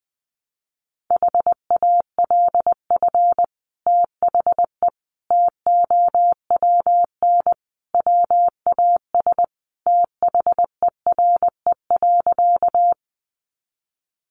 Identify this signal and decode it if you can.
Morse code, heard as a keyed tone, about 20 wpm, 700 Hz. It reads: HALF THE TOWN WAS THERE.
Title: Sentences from Top 500 Words 20wpm